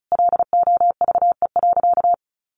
Morse code, heard as a keyed tone, about 34 wpm, 700 Hz. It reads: LOVE.
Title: Sentences from Top 1300 Words 35wpm